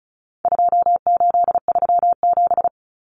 Morse code, heard as a keyed tone, about 34 wpm, 700 Hz. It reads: 2837
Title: Common Cut Numbers - 4-Digit 35wpm